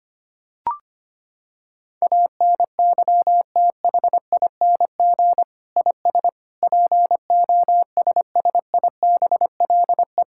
Morse code, heard as a keyed tone, about 25 wpm, 700 Hz. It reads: ANYTHING IS POSSIBLE
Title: Sentences from Top 1000 Words - Review Part 6 - 25wpm